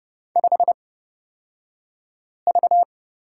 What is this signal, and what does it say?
Morse code, heard as a keyed tone, about 30 wpm, spaced as for 5 wpm, 700 Hz. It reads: 5V
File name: Lesson 38d - Letter q - Two Characters 30wpm